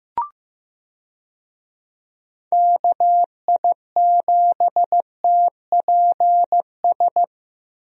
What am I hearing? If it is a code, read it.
Morse code, heard as a keyed tone, about 15 wpm, 700 Hz. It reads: KI7TPS
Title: USA Call Signs - 6 Characters - Repeated 2x - 15wpm